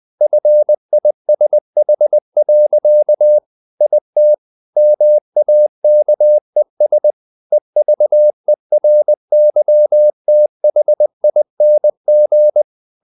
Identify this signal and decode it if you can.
Morse code, heard as a keyed tone, about 20 wpm, 600 Hz. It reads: FISH. IT MAKES EVERYTHING
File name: the old man and the sea part 2 20wpm